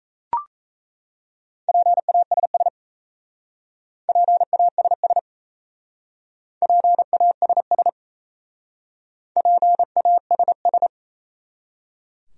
Morse code, heard as a keyed tone, about 37 wpm, 700 Hz. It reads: PASS PASS PASS PASS